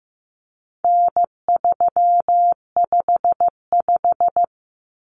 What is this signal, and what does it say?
Morse code, heard as a keyed tone, about 15 wpm, 700 Hz. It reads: N355
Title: Common Cut Numbers - 4-Digit 15wpm